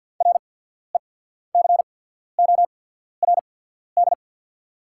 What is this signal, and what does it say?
Morse code, heard as a keyed tone, about 48 wpm, spaced as for 14 wpm, 690 Hz. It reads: RECORD
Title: 8x Character Spacing - Top 500 Words - 50wpm